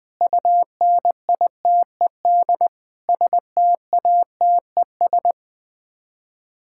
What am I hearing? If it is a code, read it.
Morse code, heard as a keyed tone, about 20 wpm, 700 Hz. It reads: UNITED STATES
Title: Sets of 6 Words - 20wpm